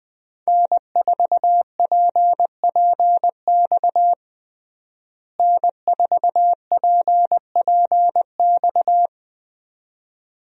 Morse code, heard as a keyed tone, about 20 wpm, 700 Hz. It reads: N4PPX N4PPX
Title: USA Call Signs - 5 Characters - Repeated 2x - 20wpm